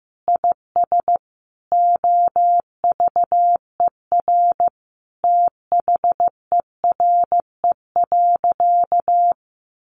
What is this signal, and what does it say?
Morse code, heard as a keyed tone, about 15 wpm, 700 Hz. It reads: IS OVER THERE.
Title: Sentences from Top 1100 Words 15wpm